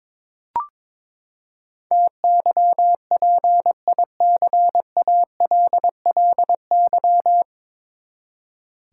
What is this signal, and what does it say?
Morse code, heard as a keyed tone, about 21 wpm, 700 Hz. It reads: TYPICALLY
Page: Morse Code Ninja practice 22wpm